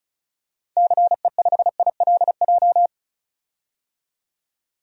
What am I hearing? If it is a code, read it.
Morse code, heard as a keyed tone, about 36 wpm, 690 Hz. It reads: CE5ILJ
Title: DXCC-100 Call Signs - 6 Characters - 35wpm